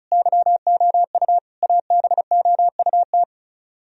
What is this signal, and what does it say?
Morse code, heard as a keyed tone, about 43 wpm, 700 Hz. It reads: YOU ABOUT
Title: Speed-Racing - Sets of 5 Words 35wpm